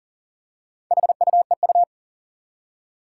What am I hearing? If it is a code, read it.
Morse code, heard as a keyed tone, about 40 wpm, 690 Hz. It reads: HUEU
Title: Four Challenging Letter-Number Combinations 40wpm